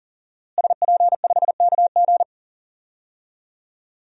Morse code, heard as a keyed tone, about 39 wpm, 700 Hz. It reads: SP5KG
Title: DXCC-100 Call Signs - 5 Characters - Repeated 2x - 40wpm